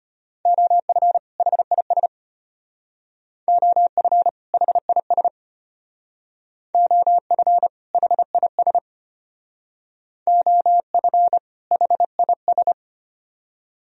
Morse code, heard as a keyed tone, about 32 wpm, 700 Hz. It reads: OF HIS OF HIS OF HIS OF HIS